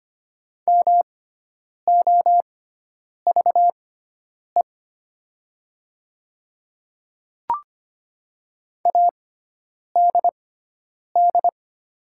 Morse code, heard as a keyed tone, about 25 wpm, spaced as for 9 wpm, 700 Hz. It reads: MOVE ADD
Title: 6x Character Spacing - Top 200 Words - 25wpm